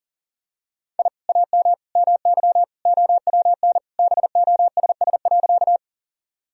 Morse code, heard as a keyed tone, about 40 wpm, 700 Hz. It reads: I AM MY OWN BOSS.
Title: Sentences from Top 1000 Words - Review Part 4 - 40wpm